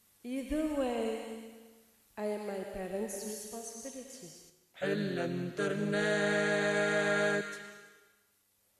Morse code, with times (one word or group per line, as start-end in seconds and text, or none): none